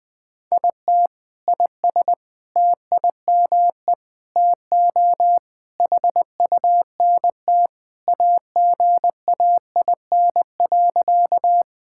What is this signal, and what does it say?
Morse code, heard as a keyed tone, about 21 wpm, 700 Hz. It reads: IT IS TIME TO HUNT AGAIN.
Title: Sentences from Top 1600 Words 20wpm